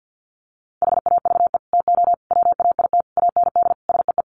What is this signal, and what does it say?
Morse code, heard as a keyed tone, about 48 wpm, 700 Hz. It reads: HAVE TO WAIT AND SEE
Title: Sets of 5 Words Encore2 50wpm